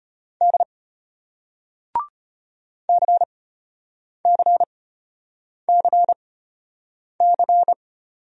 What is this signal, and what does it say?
Morse code, heard as a keyed tone, about 33 wpm, spaced as for 8 wpm, 700 Hz. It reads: D CCCC